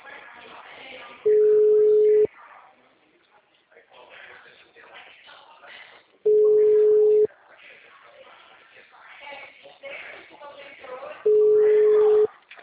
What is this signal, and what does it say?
Ringback tone (Europe, 425 Hz), with speech.